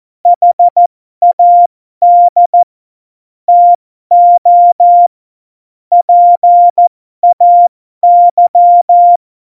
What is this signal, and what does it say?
Morse code, heard as a keyed tone, about 14 wpm, spaced as for 12 wpm, 700 Hz. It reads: HAD TO PAY
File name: Morsekurs / Witze